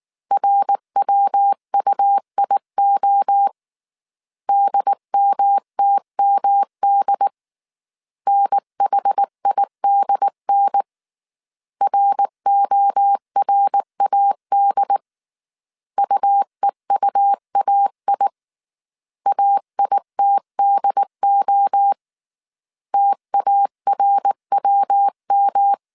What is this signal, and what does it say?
Morse code, heard as a keyed tone, about 19 wpm, 800 Hz. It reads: RWUIO DMTMD NHIDN RORAD UEUAI AITDO TARWM
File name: CW / DET / 1 3 1500C S B 18wpm 1916